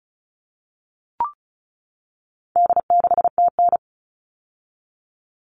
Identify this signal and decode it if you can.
Morse code, heard as a keyed tone, about 34 wpm, 700 Hz. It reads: D6TN
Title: All Cut Numbers - 4-Digit 35wpm